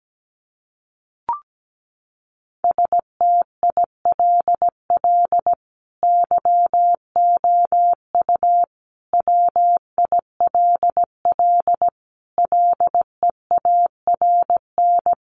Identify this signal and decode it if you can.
Morse code, heard as a keyed tone, about 17 wpm, 700 Hz. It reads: STILL YOU WILL LEARN
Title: Sentences from Top 600 Words 17wpm